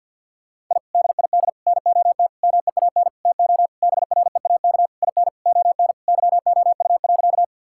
Morse code, heard as a keyed tone, about 50 wpm, 700 Hz. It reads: I DID NOT MEAN TO BREAK IN ON YOU.
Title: Sentences from Top 1000 Words - Review Part 6 - 50wpm